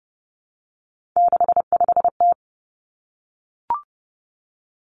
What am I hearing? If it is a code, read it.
Morse code, heard as a keyed tone, about 30 wpm, 700 Hz. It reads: 65T
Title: Three Challenging Letter-Number Combinations 30wpm